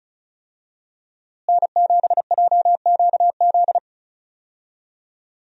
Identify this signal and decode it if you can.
Morse code, heard as a keyed tone, about 36 wpm, 700 Hz. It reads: N7JQZ
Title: USA Call Signs - 5 Characters - Repeated 3x - 35wpm